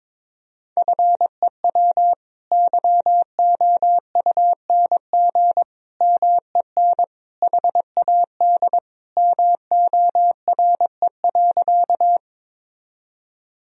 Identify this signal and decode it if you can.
Morse code, heard as a keyed tone, about 22 wpm, 700 Hz. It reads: FEW YOUNG MEN HAD MORE.
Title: Sentences from Top 1000 Words - Review Part 1 - 22wpm